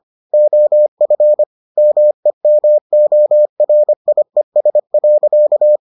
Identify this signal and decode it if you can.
Morse code, heard as a keyed tone, about 25 wpm, 600 Hz. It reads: OF MEMORIES.